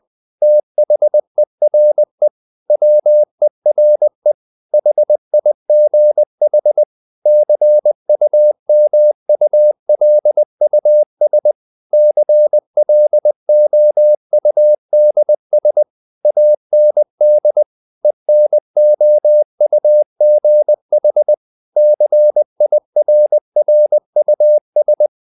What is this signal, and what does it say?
Morse code, heard as a keyed tone, about 20 wpm, 600 Hz. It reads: THERE WERE HIGH CUMULUS CLOUDS AND ENOUGH CIRRUS
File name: the old man and the sea part 2 20wpm